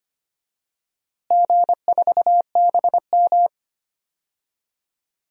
Morse code, heard as a keyed tone, about 25 wpm, 700 Hz. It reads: G4BM
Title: DXCC-100 Call Signs - 4 Characters - Repeated 2x - 25wpm